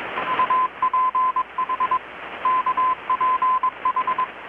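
Morse code, heard as a keyed tone, about 21 wpm, 1020 Hz, with radio noise. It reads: UPH KPH
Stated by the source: Morse from KPH's Night of Nights XX maritime radiotelegraphy event, received in Chiapas, Mexico about 3,800 km away on a Kenwood R-600 with a grounded 10 m long-wire antenna.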